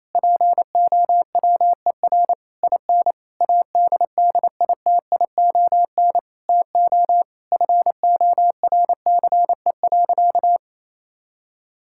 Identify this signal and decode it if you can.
Morse code, heard as a keyed tone, about 28 wpm, 700 Hz. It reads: POWER IN ADDITION TO FORCE.
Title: Sentences from Top 1000 Words - Review Part 6 - 28wpm